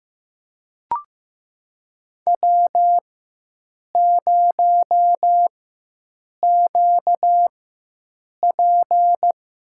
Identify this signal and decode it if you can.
Morse code, heard as a keyed tone, about 15 wpm, spaced as for 7 wpm, 700 Hz. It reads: W0QP
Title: USA Call Signs - 4 Characters - 4x Character Spacing - 15wpm